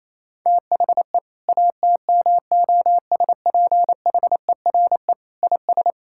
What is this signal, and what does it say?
Morse code, heard as a keyed tone, about 28 wpm, 700 Hz. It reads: THE ATMOSPHERE IS
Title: Sentences from Top 1600 Words 28wpm